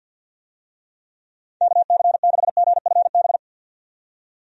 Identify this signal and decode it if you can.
Morse code, heard as a keyed tone, about 46 wpm, 700 Hz. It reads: KC6GFB
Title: USA Call Signs - 6 Characters - 50wpm